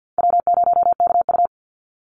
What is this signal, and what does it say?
Morse code, heard as a keyed tone, about 50 wpm, 700 Hz. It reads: W0KU
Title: Easy Call Signs - 50wpm